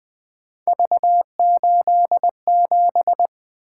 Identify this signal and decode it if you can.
Morse code, heard as a keyed tone, about 20 wpm, 700 Hz. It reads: V87
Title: Three Letter-Number Combinations - 20wpm